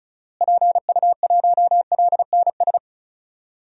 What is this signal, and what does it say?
Morse code, heard as a keyed tone, about 34 wpm, 700 Hz. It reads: PU1LNS
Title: DXCC-100 Call Signs - 6 Characters - Repeated 3x - 35wpm